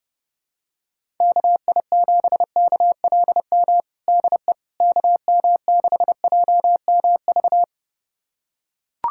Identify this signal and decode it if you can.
Morse code, heard as a keyed tone, about 32 wpm, 700 Hz. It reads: KI7KLM DE KM6JMV
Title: USA Callsign DE Callsign - 6 Characters - 30wpm